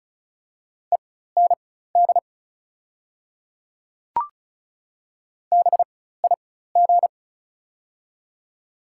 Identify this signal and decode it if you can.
Morse code, heard as a keyed tone, about 36 wpm, spaced as for 16 wpm, 700 Hz. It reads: END BIG
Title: 4x Character Spacing - Top 300 Words - 35wpm